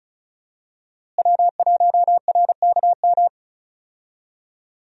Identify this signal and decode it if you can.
Morse code, heard as a keyed tone, about 34 wpm, 700 Hz. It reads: W1RKM
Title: USA Call Signs - 5 Characters - Repeated 2x - 35wpm